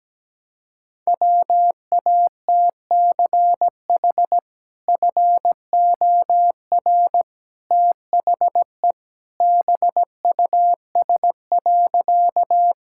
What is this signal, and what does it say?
Morse code, heard as a keyed tone, about 17 wpm, 700 Hz. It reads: WATCH FOR THE BUS.
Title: Sentences from Top 1000 Words - Review Part 6 - 17wpm